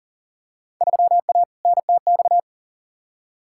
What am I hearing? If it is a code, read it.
Morse code, heard as a keyed tone, about 39 wpm, 700 Hz. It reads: 3A NTX
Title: Field Day Exchanges - 40wpm